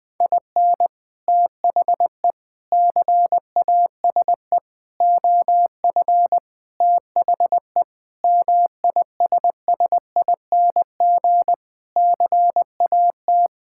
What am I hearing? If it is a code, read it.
Morse code, heard as a keyed tone, about 20 wpm, 700 Hz. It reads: IN THE CASE OF THE MISSING CAT
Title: Sentences with 4-Word Phrases - 20wpm